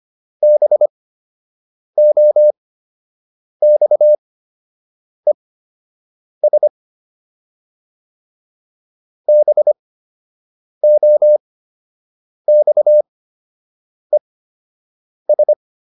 Morse code, heard as a keyed tone, about 25 wpm, spaced as for 7 wpm, 600 Hz. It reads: BOXES BOXES